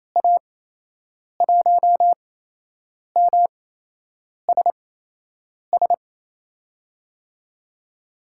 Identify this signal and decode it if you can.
Morse code, heard as a keyed tone, about 28 wpm, spaced as for 8 wpm, 700 Hz. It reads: A1MSS
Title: USA Call Signs - 6 Characters - 8x Character Spacing - 28wpm